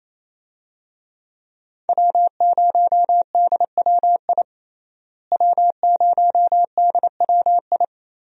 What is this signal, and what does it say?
Morse code, heard as a keyed tone, about 28 wpm, 700 Hz. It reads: W0DWI W0DWI
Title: USA Call Signs - 5 Characters - Repeated 2x - 28wpm